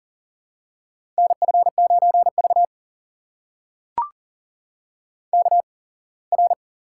Morse code, heard as a keyed tone, about 40 wpm, spaced as for 24 wpm, 700 Hz. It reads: NF9V KR
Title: USA Call Signs - 8x Character Spacing - 40wpm